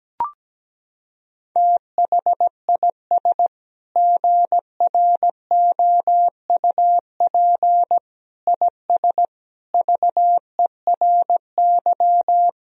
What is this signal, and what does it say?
Morse code, heard as a keyed tone, about 17 wpm, 700 Hz. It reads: THIS GROUP IS VERY
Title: Sentences from Top 1000 Words - Review Part 1 - 17wpm